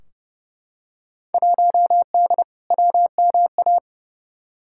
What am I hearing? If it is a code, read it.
Morse code, heard as a keyed tone, about 30 wpm, 700 Hz. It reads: 1D WMA
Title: Field Day Exchanges - 30wpm